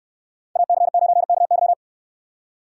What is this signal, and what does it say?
Morse code, heard as a keyed tone, about 69 wpm, 700 Hz. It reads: AC9KY